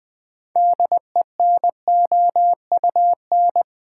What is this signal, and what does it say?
Morse code, heard as a keyed tone, about 20 wpm, 700 Hz. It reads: DENOUN
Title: Build A Word - 2 Letters At A Time - 8-Letter Words - 20wpm